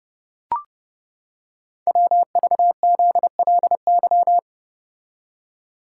Morse code, heard as a keyed tone, about 30 wpm, 700 Hz. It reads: WVZLY
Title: Code Groups - 5 Letters - 30wpm